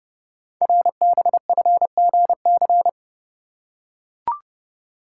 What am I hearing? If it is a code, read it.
Morse code, heard as a keyed tone, about 30 wpm, 700 Hz. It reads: RBFGC
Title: Code Groups - 5 Letters - 30wpm